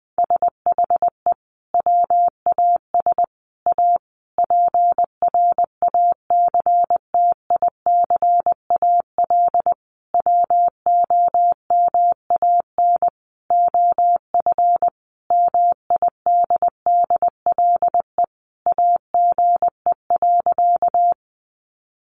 Morse code, heard as a keyed tone, about 20 wpm, 700 Hz. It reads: SHE WAS A PRACTICAL WOMAN OF MIDDLE AGE.
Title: Sentences from Top 1400 Words 20wpm